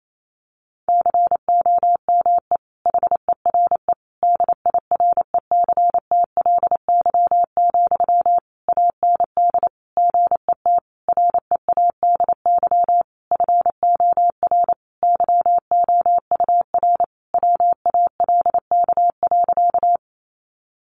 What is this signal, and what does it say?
Morse code, heard as a keyed tone, about 28 wpm, 700 Hz. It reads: COME HERE DIRECTLY, AND GET READY FOR YOUR WALK.
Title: Sentences from Top 800 Words 28wpm